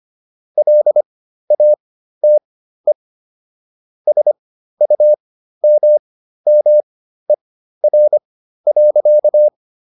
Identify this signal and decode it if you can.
Morse code, heard as a keyed tone, about 25 wpm, spaced as for 13 wpm, 600 Hz. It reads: LATE SUMMER.